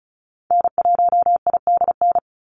Morse code, heard as a keyed tone, about 36 wpm, 700 Hz. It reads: N1IDN
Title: NA SOTA Activator Callsigns 35wpm